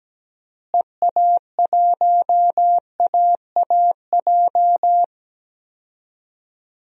Morse code, heard as a keyed tone, about 17 wpm, 700 Hz. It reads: EA1AAJ
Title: DXCC-100 Call Signs - 6 Characters - Repeated 2x - 17wpm